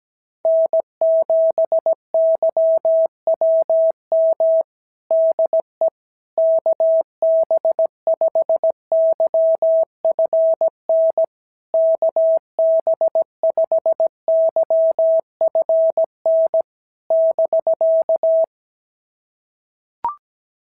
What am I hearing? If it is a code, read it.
Morse code, heard as a keyed tone, about 16 wpm, 650 Hz. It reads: N7YWM DE KB5YFN KB5YFN <BK>
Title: SOTA QSO Practice 17wpm